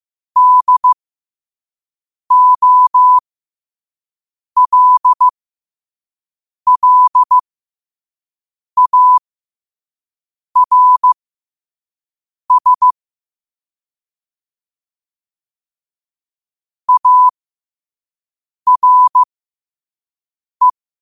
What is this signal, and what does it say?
Morse code, heard as a keyed tone, about 15 wpm, spaced as for 5 wpm, 1000 Hz. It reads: DOLLARS ARE
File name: qotdfiles / 5wpm